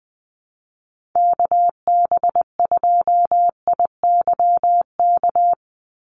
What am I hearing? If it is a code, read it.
Morse code, heard as a keyed tone, about 20 wpm, 700 Hz. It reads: KB2IYK
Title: USA Call Signs 20wpm